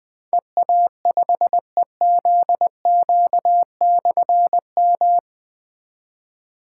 Morse code, heard as a keyed tone, about 20 wpm, 700 Hz. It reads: EA5EZQ/M
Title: Difficult Call Signs 20wpm